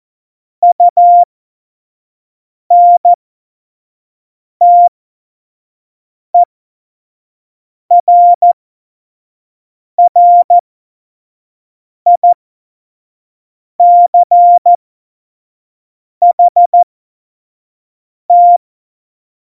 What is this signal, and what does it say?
Morse code, heard as a keyed tone, about 14 wpm, spaced as for 5 wpm, 700 Hz. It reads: UNTERRICHT
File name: Morsekurs / Witze / Witz 11. Juni 2019, 25 F70